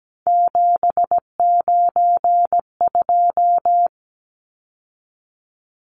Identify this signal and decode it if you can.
Morse code, heard as a keyed tone, about 17 wpm, 700 Hz. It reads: 792
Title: Numbers - 3-Digit 17wpm